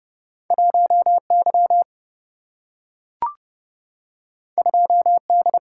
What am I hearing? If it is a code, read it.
Morse code, heard as a keyed tone, about 30 wpm, 700 Hz. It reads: 1 Y 2 D
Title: Single-Letter Single-Number 30wpm